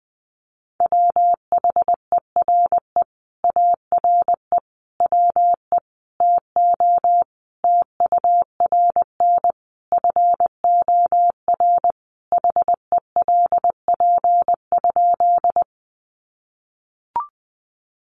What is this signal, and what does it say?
Morse code, heard as a keyed tone, about 20 wpm, 700 Hz. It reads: WHERE ARE WE TO TURN FOR HELP?
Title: Sentences from Top 200 Words 20wpm